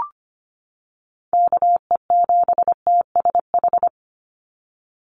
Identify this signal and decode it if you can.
Morse code, heard as a keyed tone, about 25 wpm, 700 Hz. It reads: KE7TSH
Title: USA Call Signs 25wpm